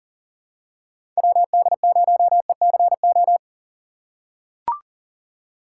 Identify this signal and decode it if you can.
Morse code, heard as a keyed tone, about 40 wpm, 700 Hz. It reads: WD0ECO
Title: K1USN SST QSO Practice - 40wpm